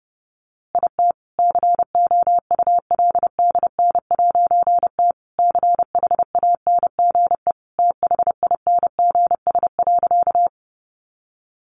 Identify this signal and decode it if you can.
Morse code, heard as a keyed tone, about 30 wpm, 700 Hz. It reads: IT COULDN'T CHANGE THINGS.
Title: A Sound Of Thunder 30wpm